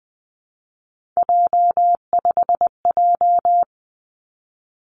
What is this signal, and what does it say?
Morse code, heard as a keyed tone, about 19 wpm, 700 Hz. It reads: J5J
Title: Three Challenging Letter-Number Combinations 20wpm